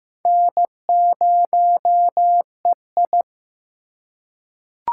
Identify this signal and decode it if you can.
Morse code, heard as a keyed tone, about 15 wpm, 700 Hz. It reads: N0EI
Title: USA Call Signs - 4 Characters - 4x Character Spacing - 15wpm